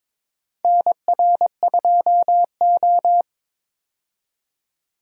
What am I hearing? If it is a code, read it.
Morse code, heard as a keyed tone, about 22 wpm, 700 Hz. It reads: NR2O
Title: USA Call Signs - 4 Characters - 22wpm